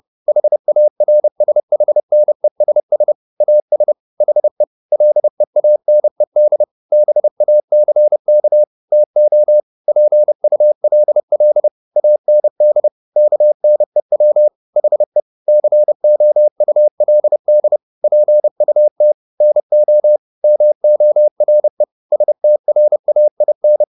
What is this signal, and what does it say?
Morse code, heard as a keyed tone, about 30 wpm, 600 Hz. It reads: HARSHNESS AS HE LEANED BACK TO PULL AND KNEW HE COULD PUT NO MORE STRAIN
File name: the old man and the sea / the old man and the sea 1 30wpm